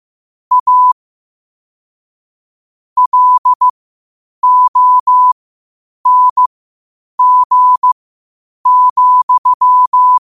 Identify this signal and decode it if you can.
Morse code, heard as a keyed tone, about 15 wpm, spaced as for 8 wpm, 1000 Hz. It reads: A LONG,
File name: qotdfiles / 8wpm